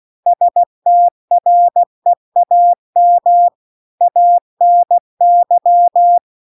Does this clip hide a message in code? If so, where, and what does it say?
Yes: Morse code from 0.3 to 6.2 s, heard as a keyed tone, about 16 wpm, 700 Hz. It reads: STREAM ANY